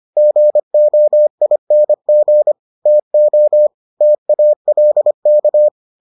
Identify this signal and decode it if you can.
Morse code, heard as a keyed tone, about 25 wpm, 600 Hz. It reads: GOING TO TALK